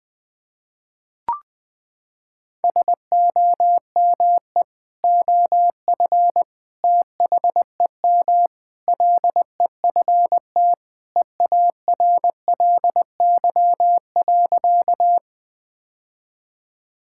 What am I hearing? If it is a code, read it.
Morse code, heard as a keyed tone, about 21 wpm, 700 Hz. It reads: SOME OF THEM LEFT EARLY.
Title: Sentences with 3-Word Phrases - 20wpm